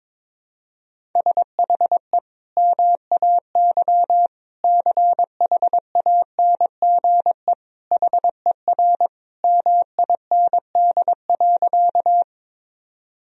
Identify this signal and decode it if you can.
Morse code, heard as a keyed tone, about 22 wpm, 700 Hz. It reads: SHE MAY CHANGE HER MIND.
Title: Sentences from Top 1000 Words - Review Part 1 - 22wpm